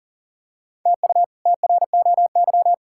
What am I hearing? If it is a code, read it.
Morse code, heard as a keyed tone, about 37 wpm, 700 Hz. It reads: TU TROY
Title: K1USN SST QSO Practice - 40wpm